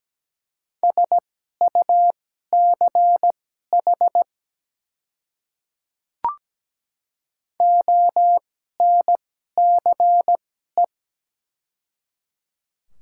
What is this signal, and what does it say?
Morse code, heard as a keyed tone, about 17 wpm, spaced as for 12 wpm, 700 Hz. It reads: SUCH ONCE
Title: 2x Character Spacing - Top 300 Words - 17wpm